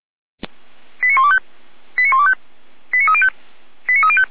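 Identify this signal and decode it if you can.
Five-tone selective call (ZVEI), heard as quick runs of five tones, tones of about 70 ms. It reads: 89125 89125 89305 89305
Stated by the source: ZVEI five-tone selective call received on an SDR, from the Feuerwehr Orlamünde fire brigade.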